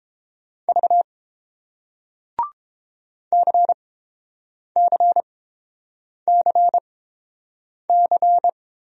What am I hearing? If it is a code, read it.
Morse code, heard as a keyed tone, about 30 wpm, spaced as for 7 wpm, 700 Hz. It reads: V CCCC